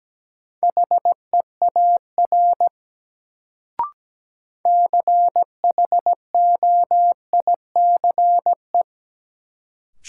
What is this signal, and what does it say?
Morse code, heard as a keyed tone, about 17 wpm, 700 Hz. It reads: HEAR CHOICE
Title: Sets of 5 Words - New Word Drill - 17wpm